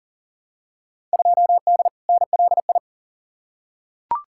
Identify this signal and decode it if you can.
Morse code, heard as a keyed tone, about 40 wpm, 700 Hz. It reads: 2D NLI
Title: Field Day Exchanges Encore - 40wpm